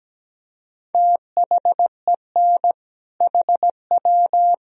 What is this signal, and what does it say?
Morse code, heard as a keyed tone, about 17 wpm, 700 Hz. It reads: THEN HW
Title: Sentences from Top 100 Words and Abbreviations 17wpm